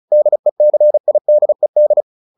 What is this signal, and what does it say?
Morse code, heard as a keyed tone, about 34 wpm, 600 Hz. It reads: DECIDED